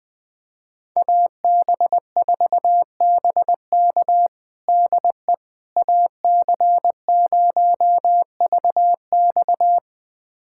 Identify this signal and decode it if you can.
Morse code, heard as a keyed tone, about 21 wpm, 700 Hz. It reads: AB4BK DE AC0VX
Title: USA Callsign DE Callsign - 5 Characters - 20wpm